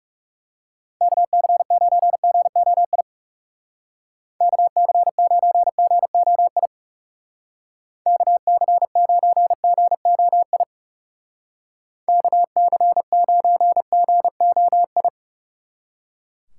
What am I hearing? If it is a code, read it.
Morse code, heard as a keyed tone, about 36 wpm, 700 Hz. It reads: KC9GOI KC9GOI KC9GOI KC9GOI
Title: Speed Racing - USA 2x3 Call Signs 30wpm